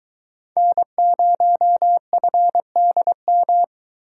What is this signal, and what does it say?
Morse code, heard as a keyed tone, about 23 wpm, 700 Hz. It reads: N0FDM